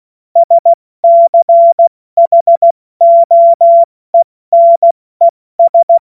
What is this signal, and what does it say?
Morse code, heard as a keyed tone, about 16 wpm, 680 Hz. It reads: SCHOENES
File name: Morsekurs / Witze / Witz 16. November 2021, 70 F80